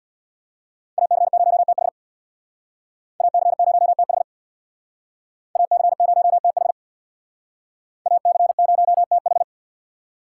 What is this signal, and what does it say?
Morse code, heard as a keyed tone, about 64 wpm, 700 Hz. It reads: AC0TH AC0TH AC0TH AC0TH